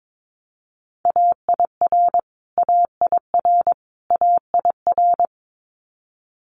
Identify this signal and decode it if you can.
Morse code, heard as a keyed tone, about 22 wpm, 700 Hz. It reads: AIR AIR AIR
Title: Top 200 Words Repeated 22wpm